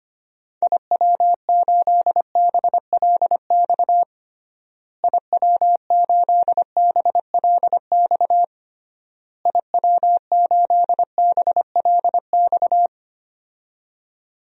Morse code, heard as a keyed tone, about 25 wpm, 700 Hz. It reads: IW8BLX IW8BLX IW8BLX
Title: DXCC-100 Call Signs - 6 Characters - Repeated 3x - 25wpm